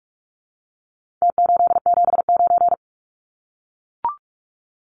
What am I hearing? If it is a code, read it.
Morse code, heard as a keyed tone, about 44 wpm, 700 Hz. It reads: T879
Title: Numbers - Contest Style - 4-Digit 45wpm